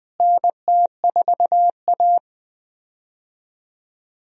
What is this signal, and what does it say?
Morse code, heard as a keyed tone, about 20 wpm, 700 Hz. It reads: NT4A
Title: USA Call Signs - 4 Characters - Repeated 2x - 20wpm